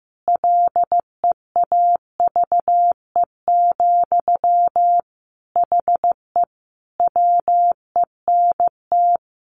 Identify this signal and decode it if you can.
Morse code, heard as a keyed tone, about 14 wpm, 700 Hz. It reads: LEAVE, HE WENT
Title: Sentences from Top 1200 Words 15wpm